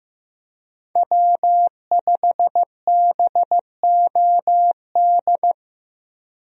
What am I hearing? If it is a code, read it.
Morse code, heard as a keyed tone, about 15 wpm, 700 Hz. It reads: W5BOD
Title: USA Call Signs - 5 Characters - Repeated 3x - 15wpm